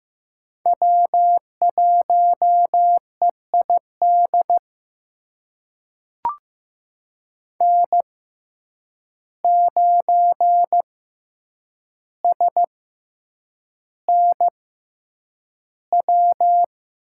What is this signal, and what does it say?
Morse code, heard as a keyed tone, about 15 wpm, spaced as for 8 wpm, 700 Hz. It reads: W1EID N9SNW
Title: USA Call Signs - 5 Characters - 6x Character Spacing - 15wpm